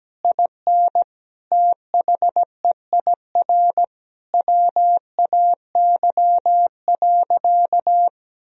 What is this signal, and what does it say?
Morse code, heard as a keyed tone, about 17 wpm, 700 Hz. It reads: IN THEIR WAY.
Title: Sentences from Top 1000 Words - Review Part 1 - 17wpm